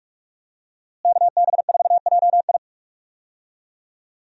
Morse code, heard as a keyed tone, about 47 wpm, 700 Hz. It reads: KB4JI
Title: USA Call Signs - 5 Characters - Repeated 2x - 45wpm